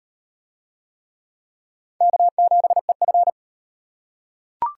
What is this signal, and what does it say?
Morse code, heard as a keyed tone, about 34 wpm, 700 Hz. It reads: K7EF